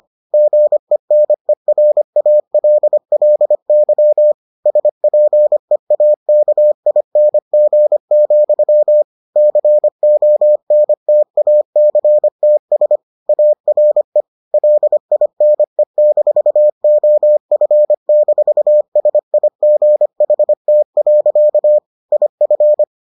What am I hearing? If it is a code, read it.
Morse code, heard as a keyed tone, about 25 wpm, 600 Hz. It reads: GENERALLY SPEAKING, CONTACTS ARE LINE-OF-SIGHT. IF